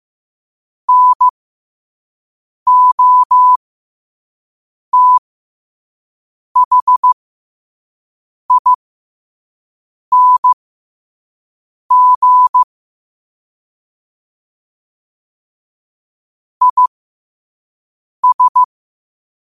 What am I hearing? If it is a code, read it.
Morse code, heard as a keyed tone, about 15 wpm, spaced as for 5 wpm, 1000 Hz. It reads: NOTHING IS
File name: qotdfiles / 5wpm